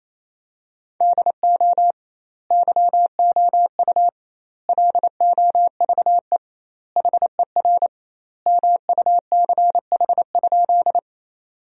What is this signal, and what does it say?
Morse code, heard as a keyed tone, about 28 wpm, 700 Hz. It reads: DO YOU LOVE HER MUCH?
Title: 2x Word-Spacing - Sentences from Top 400 Words - 28wpm